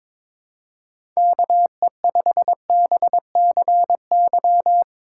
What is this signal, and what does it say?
Morse code, heard as a keyed tone, about 22 wpm, 700 Hz. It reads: KE5BCY
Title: K1USN SST QSO Practice - 22wpm